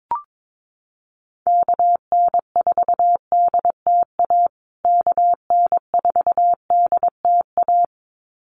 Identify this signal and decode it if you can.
Morse code, heard as a keyed tone, about 22 wpm, 700 Hz. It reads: KN4DTA KN4DTA
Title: USA Call Signs Repeated 2x 22wpm